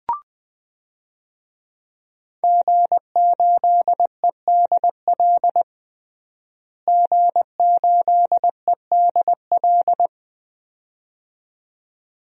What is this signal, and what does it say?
Morse code, heard as a keyed tone, about 21 wpm, 700 Hz. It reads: G8EDL G8EDL
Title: DXCC-100 Call Signs - 5 Characters - Repeated 2x - 20wpm